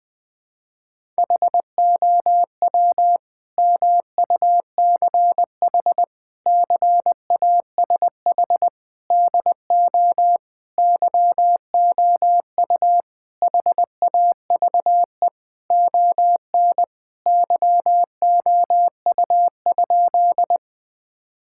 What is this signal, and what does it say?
Morse code, heard as a keyed tone, about 20 wpm, 700 Hz. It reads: HOW MUCH CASH DO YOU HAVE ON YOU?
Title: Sentences from Top 1000 Words - Review Part 5 - 20wpm